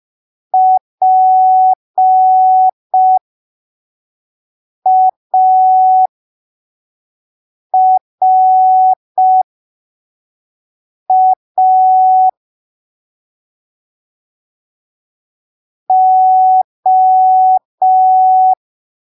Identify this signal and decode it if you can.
Morse code, heard as a keyed tone, about 5 wpm, spaced as for 3 wpm, 750 Hz. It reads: PARA O
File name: arquivos / texto / texto 08 05ppm